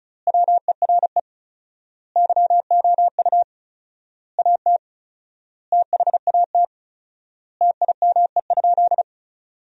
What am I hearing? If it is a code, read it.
Morse code, heard as a keyed tone, about 34 wpm, 700 Hz. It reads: WERE YOU AT THAT TIME?
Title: 4x Word-Spacing - Sentences from Top 200 Words - 35wpm